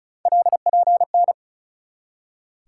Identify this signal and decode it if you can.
Morse code, heard as a keyed tone, about 36 wpm, 700 Hz. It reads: LPN
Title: K1USN SST QSO Practice 35wpm